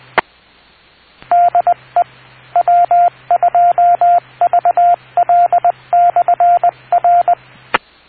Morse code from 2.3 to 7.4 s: W2VL/R